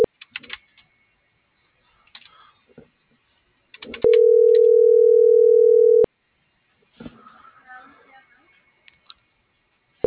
Ringback tone (North America), 440 + 480 Hz.